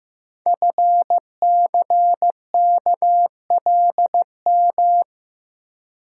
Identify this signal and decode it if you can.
Morse code, heard as a keyed tone, about 15 wpm, 700 Hz. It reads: FCKLM
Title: Code Groups - 5 Letters - 15wpm